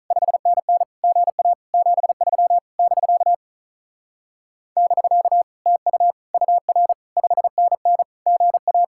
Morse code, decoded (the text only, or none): 5NN GA 73 <BK> <BK> TU UR 5NN GA